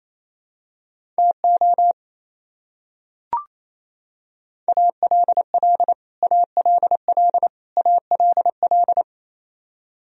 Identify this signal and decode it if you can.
Morse code, heard as a keyed tone, about 28 wpm, 700 Hz. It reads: TO ALL ALL ALL
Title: Top 100 Words Repeated 28wpm